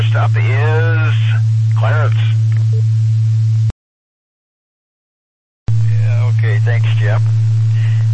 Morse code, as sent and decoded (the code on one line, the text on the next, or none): none